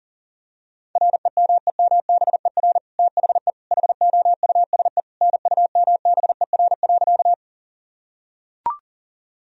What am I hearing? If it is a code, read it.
Morse code, heard as a keyed tone, about 40 wpm, 700 Hz. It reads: REMEMBER THE HOUSE NUMBER.
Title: Sentences from Top 1000 Words - Review Part 1 - 40wpm